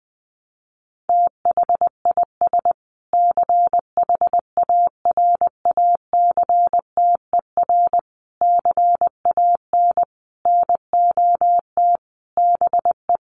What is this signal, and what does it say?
Morse code, heard as a keyed tone, about 20 wpm, 700 Hz. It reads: THIS CHARACTER CAN NOT BE
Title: Sets of 5 Words Encore2 20wpm